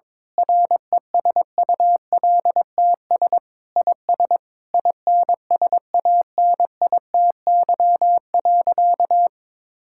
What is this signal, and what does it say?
Morse code, heard as a keyed tone, about 22 wpm, 700 Hz. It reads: RESULTS IS INSANITY.
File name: Sentences with 9-Word Phrases - 22wpm